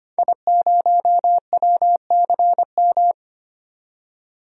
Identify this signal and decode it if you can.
Morse code, heard as a keyed tone, about 25 wpm, 700 Hz. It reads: I0WCM
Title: DXCC-100 Call Signs - 5 Characters - 25wpm